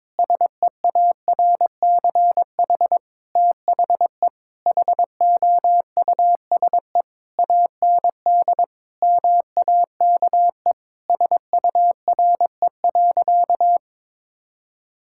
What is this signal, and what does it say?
Morse code, heard as a keyed tone, about 21 wpm, 700 Hz. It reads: SEARCH THE HOUSE AND MAKE SURE.
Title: Sentences from Top 1000 Words - Review Part 5 - 22wpm